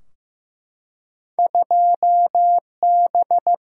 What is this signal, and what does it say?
Morse code, heard as a keyed tone, about 15 wpm, 700 Hz. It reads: 2B
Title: Single-Letter Single-Number 15wpm